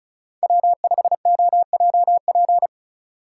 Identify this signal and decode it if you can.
Morse code, heard as a keyed tone, about 36 wpm, 690 Hz. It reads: W5OJP
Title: USA Call Signs - 5 Characters - 3x Character Spacing - 35wpm